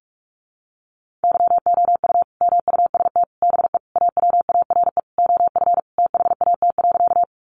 Morse code, heard as a keyed tone, about 44 wpm, 700 Hz. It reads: YOU MUST BE AWARE OF THAT.
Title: Sentences from Top 1200 Words 45wpm